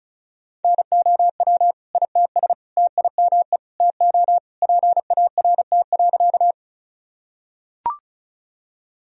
Morse code, heard as a keyed tone, about 36 wpm, 700 Hz. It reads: NOW ITS TIME TO PART.
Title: Sentences from Top 1000 Words - Review Part 1 - 35wpm